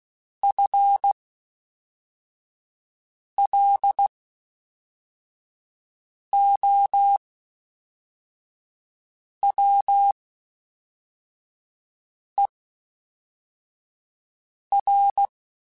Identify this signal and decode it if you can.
Morse code, heard as a keyed tone, about 16 wpm, spaced as for 4 wpm, 800 Hz. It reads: FLOWER
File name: Spaced / FL Runs sp mp3s / FL Runs sp @5